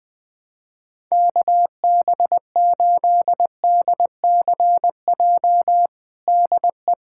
Morse code, heard as a keyed tone, about 20 wpm, 700 Hz. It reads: KB8DCJ DE
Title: USA Callsign DE Callsign - 6 Characters - 20wpm